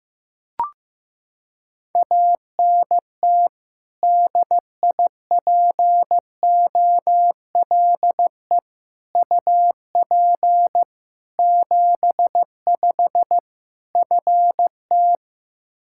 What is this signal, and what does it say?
Morse code, heard as a keyed tone, about 15 wpm, 700 Hz. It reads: ANT DIPOLE UP 75 FT
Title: Practice QSO - Exchange 2 - Info - 15wpm